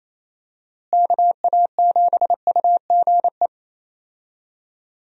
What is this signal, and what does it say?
Morse code, heard as a keyed tone, about 27 wpm, 700 Hz. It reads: KA7UGE
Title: USA Call Signs - 6 Characters - Repeated 3x - 28wpm